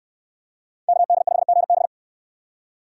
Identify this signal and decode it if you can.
Morse code, heard as a keyed tone, about 68 wpm, 700 Hz. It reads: KD5GB